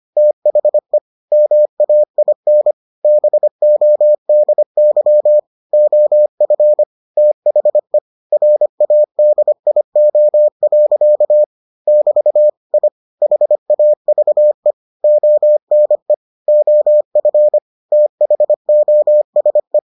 Morse code, heard as a keyed tone, about 25 wpm, 600 Hz. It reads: THE MAIN BODY OF THE RADIO. <BT> I HAVE ONE OF THOSE